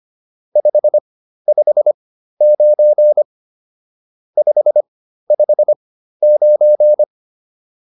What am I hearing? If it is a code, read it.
Morse code, heard as a keyed tone, about 25 wpm, spaced as for 13 wpm, 600 Hz. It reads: 559 559
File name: QSO405 13